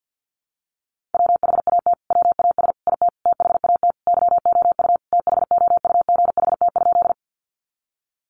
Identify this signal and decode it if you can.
Morse code, heard as a keyed tone, about 50 wpm, 700 Hz. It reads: WHAT WAS IT THAT YOU THOUGHT?
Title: Sentences from Top 300 Words 50wpm